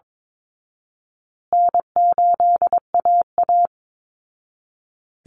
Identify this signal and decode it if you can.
Morse code, heard as a keyed tone, about 22 wpm, 700 Hz. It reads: N8AA
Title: Easy Call Signs Repeated 3x - 22wpm